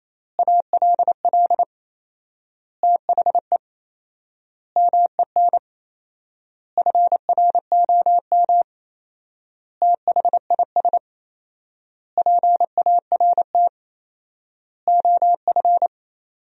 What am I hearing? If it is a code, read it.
Morse code, heard as a keyed tone, about 27 wpm, 700 Hz. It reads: ALL THE MEN FROM THIS PART OF
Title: 4x Word-Spacing - Sentences from Top 300 Words - 28wpm